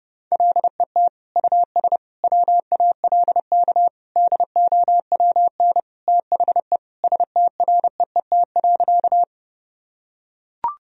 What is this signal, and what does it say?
Morse code, heard as a keyed tone, about 30 wpm, 700 Hz. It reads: LET US WALK DOWN THE STREET.
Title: Sentences from Top 1000 Words - Review Part 3 - 30wpm